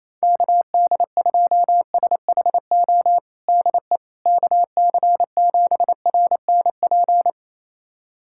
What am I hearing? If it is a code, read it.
Morse code, heard as a keyed tone, about 28 wpm, 700 Hz. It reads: KD2SHO DE KC7RNP
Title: USA Callsign DE Callsign - 6 Characters - 28wpm